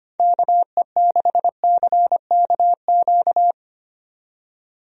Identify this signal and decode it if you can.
Morse code, heard as a keyed tone, about 25 wpm, 700 Hz. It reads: KE6CKQ
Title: USA Call Signs - 6 Characters - 25wpm